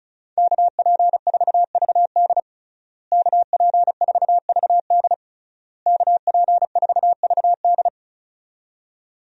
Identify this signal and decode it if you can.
Morse code, heard as a keyed tone, about 34 wpm, 700 Hz. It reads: KP4VD KP4VD KP4VD
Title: DXCC-100 Call Signs - 5 Characters - Repeated 3x - 35wpm